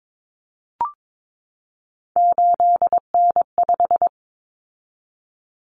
Morse code, heard as a keyed tone, about 22 wpm, 700 Hz. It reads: 8N5
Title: Common Cut Numbers - 3-Digit 22wpm